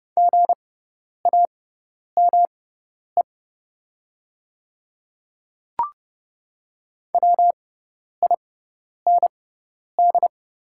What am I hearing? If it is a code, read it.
Morse code, heard as a keyed tone, about 29 wpm, spaced as for 10 wpm, 700 Hz. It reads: GAME WIND
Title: 6x Character Spacing - Top 500 Words - 30wpm